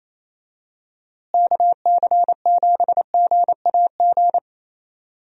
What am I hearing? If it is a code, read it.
Morse code, heard as a keyed tone, about 28 wpm, 700 Hz. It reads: KC7GAG
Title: USA Call Signs - 6 Characters - Repeated 2x - 28wpm